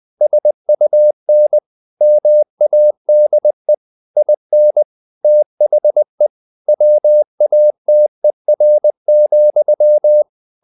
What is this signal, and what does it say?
Morse code, heard as a keyed tone, about 20 wpm, 600 Hz. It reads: SUN MADE IN THE WATER,
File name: the old man and the sea / the old man and the sea part 1 20wpm